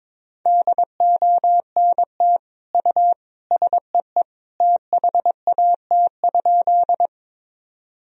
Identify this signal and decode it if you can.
Morse code, heard as a keyed tone, about 22 wpm, 700 Hz. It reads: DONT U SEE THAT?
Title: Sentences from Top 300 Words and Abbreviations 22wpm